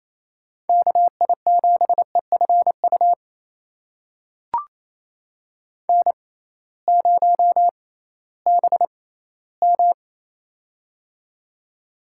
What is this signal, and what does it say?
Morse code, heard as a keyed tone, about 28 wpm, spaced as for 16 wpm, 700 Hz. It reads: KI7EFU N0BM
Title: USA Call Signs - 6x Character Spacing - 28wpm